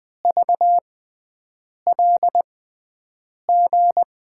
Morse code, heard as a keyed tone, about 20 wpm, spaced as for 7 wpm, 700 Hz. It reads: VLG
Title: USA Call Signs - 5 Characters - 6x Character Spacing - 20wpm